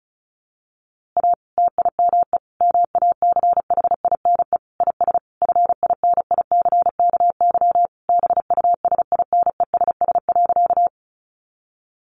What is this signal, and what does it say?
Morse code, heard as a keyed tone, about 34 wpm, 700 Hz. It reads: A TIME MACHINE IS FINICKY BUSINESS.
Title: A Sound Of Thunder 35wpm